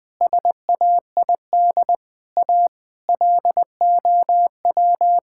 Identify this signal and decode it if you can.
Morse code, heard as a keyed tone, about 19 wpm, 700 Hz. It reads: SAID A LOW
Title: Sentences from Top 1600 Words 20wpm